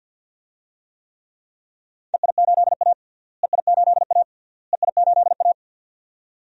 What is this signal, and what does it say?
Morse code, heard as a keyed tone, about 46 wpm, 700 Hz. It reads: EI8A EI8A EI8A
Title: DXCC-100 Call Signs - 4 Characters - Repeated 3x - 50wpm